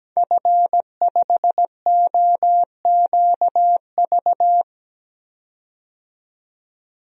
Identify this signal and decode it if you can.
Morse code, heard as a keyed tone, about 17 wpm, 700 Hz. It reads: F5OQV
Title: DXCC-100 Call Signs - 5 Characters - 17wpm